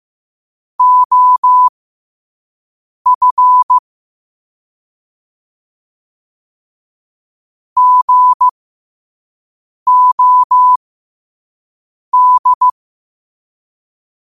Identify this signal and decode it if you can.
Morse code, heard as a keyed tone, about 15 wpm, spaced as for 5 wpm, 1000 Hz. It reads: OF GOD
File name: qotdfiles / 5wpm